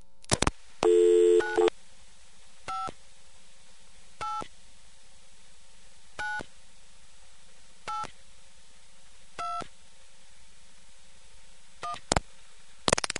DTMF, heard as telephone touch tones, tones of about 190 ms. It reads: 9 5 8 9 8 2 1